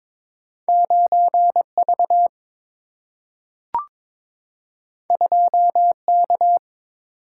Morse code, heard as a keyed tone, about 22 wpm, 700 Hz. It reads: 9 V 2 K
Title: Single-Letter Single-Number 22wpm